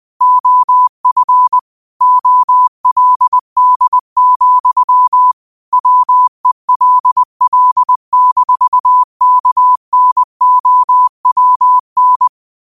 Morse code, heard as a keyed tone, about 20 wpm, 1000 Hz. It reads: OF OLD, WELL-KNOWN
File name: qotdfiles / 20wpm